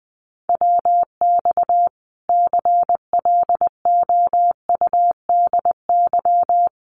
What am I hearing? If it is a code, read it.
Morse code, heard as a keyed tone, about 20 wpm, 700 Hz. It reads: WX CLOUDY
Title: Common QSO Phrases 20wpm